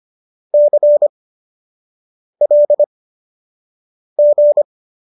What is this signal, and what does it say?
Morse code, heard as a keyed tone, about 25 wpm, spaced as for 6 wpm, 590 Hz. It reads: CLG